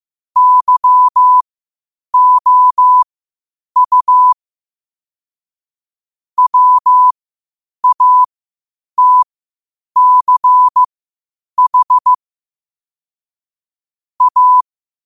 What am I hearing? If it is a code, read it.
Morse code, heard as a keyed tone, about 15 wpm, spaced as for 8 wpm, 1000 Hz. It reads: YOU WATCH A
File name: qotdfiles / 8wpm